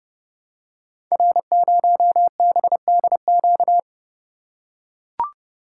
Morse code, heard as a keyed tone, about 30 wpm, 700 Hz. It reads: R0BDQ
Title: Code Groups - 5 Letter:Numbers - 30wpm